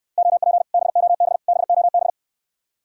Morse code, heard as a keyed tone, about 69 wpm, 700 Hz. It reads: OP BOB BOB